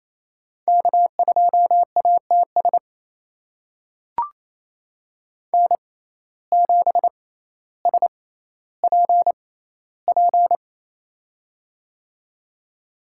Morse code, heard as a keyed tone, about 28 wpm, spaced as for 14 wpm, 700 Hz. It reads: K2ATS N7SPP
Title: USA Call Signs - 5 Characters - 6x Character Spacing - 28wpm